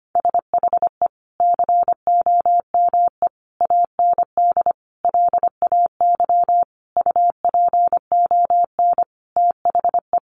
Morse code, heard as a keyed tone, about 25 wpm, 700 Hz. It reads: SHE COME AND LAY UPON THE
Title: Sentences from Top 1100 Words 25wpm